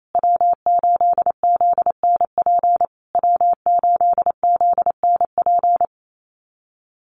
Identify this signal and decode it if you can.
Morse code, heard as a keyed tone, about 28 wpm, 700 Hz. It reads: W8ZNP W8ZNP
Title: USA Call Signs Repeated 3x 28wpm